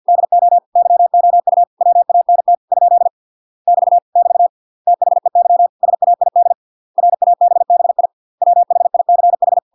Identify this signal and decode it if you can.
Morse code, heard as a keyed tone, about 50 wpm, 700 Hz. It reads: DO YOU WANT ? <BT> <BT> THEY SAID RABBI WHICH